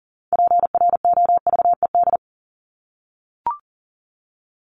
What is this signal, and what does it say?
Morse code, heard as a keyed tone, about 40 wpm, 700 Hz. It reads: PROVED
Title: ED Suffix - Top 100 Words with Base Words 40wpm